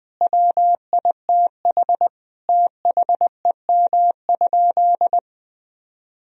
Morse code, heard as a keyed tone, about 20 wpm, 700 Hz. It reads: WITH THEM?
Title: Sentences from Top 1000 Words - Review Part 3 - 20wpm